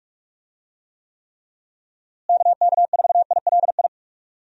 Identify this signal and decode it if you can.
Morse code, heard as a keyed tone, about 41 wpm, 700 Hz. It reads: KK4ILI
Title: USA Call Signs - 6 Characters - Repeated 3x - 45wpm